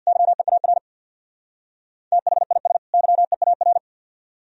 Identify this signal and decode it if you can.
Morse code, heard as a keyed tone, about 63 wpm, 700 Hz. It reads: YEAR THIS YEAR